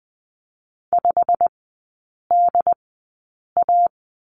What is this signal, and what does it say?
Morse code, heard as a keyed tone, about 20 wpm, spaced as for 8 wpm, 700 Hz. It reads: 5DA
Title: 087 Lesson 15e - Number 5 - Three Characters 20wpm